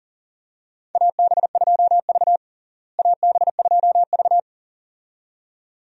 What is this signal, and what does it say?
Morse code, heard as a keyed tone, about 37 wpm, 700 Hz. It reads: AB2V AB2V
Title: USA Call Signs - 4 Characters - Repeated 2x - 40wpm